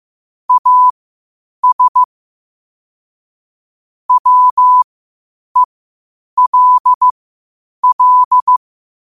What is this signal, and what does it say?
Morse code, heard as a keyed tone, about 15 wpm, spaced as for 8 wpm, 1000 Hz. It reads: AS WELL
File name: qotdfiles / 8wpm